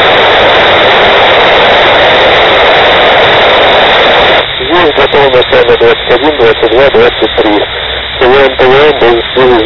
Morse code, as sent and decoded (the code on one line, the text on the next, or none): none